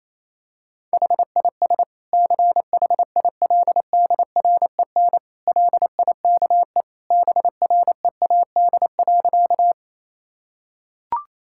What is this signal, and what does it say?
Morse code, heard as a keyed tone, about 28 wpm, 700 Hz. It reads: HIS CHILDREN LIKE BREAD.
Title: Sentences from Top 1600 Words 28wpm